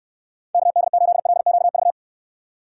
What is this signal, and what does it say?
Morse code, heard as a keyed tone, about 69 wpm, 700 Hz. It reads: KD8LQV